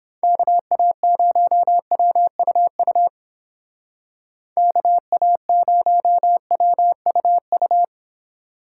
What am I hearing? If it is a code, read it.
Morse code, heard as a keyed tone, about 28 wpm, 700 Hz. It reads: KA0WUU KA0WUU